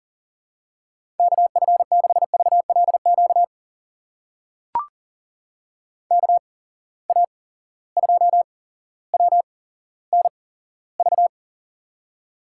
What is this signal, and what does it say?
Morse code, heard as a keyed tone, about 39 wpm, spaced as for 17 wpm, 700 Hz. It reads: KF6VLQ KA2WNV
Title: USA Call Signs - 6 Characters - 8x Character Spacing - 40wpm